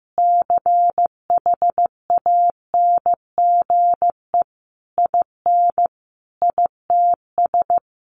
Morse code, heard as a keyed tone, about 15 wpm, 700 Hz. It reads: CHANGE IN ITS
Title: Sentences from Top 1300 Words 15wpm